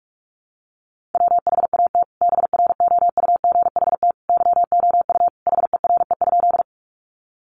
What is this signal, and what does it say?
Morse code, heard as a keyed tone, about 44 wpm, 700 Hz. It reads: WHAT BROUGHT YOU HERE?
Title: Sentences from Top 500 Words 45wpm